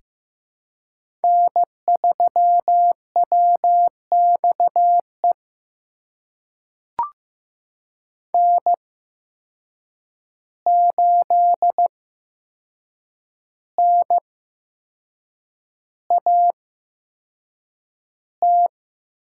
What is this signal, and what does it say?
Morse code, heard as a keyed tone, about 15 wpm, spaced as for 6 wpm, 700 Hz. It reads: N3WXE N8NAT
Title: USA Call Signs - 8x Character Spacing - 15wpm